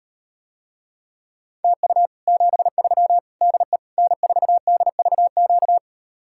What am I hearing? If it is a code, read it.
Morse code, heard as a keyed tone, about 39 wpm, 700 Hz. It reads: TU 73 DE N4DVQ